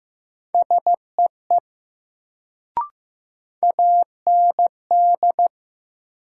Morse code, heard as a keyed tone, about 15 wpm, 700 Hz. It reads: SEE AND
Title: Sentences from Top 100 Words - New Word Drill - 15wpm